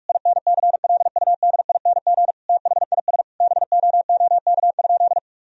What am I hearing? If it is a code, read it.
Morse code, heard as a keyed tone, about 46 wpm, 700 Hz. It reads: INCLUDING THIS BOOK?
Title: Sentences from Top 1000 Words - Review Part 4 - 45wpm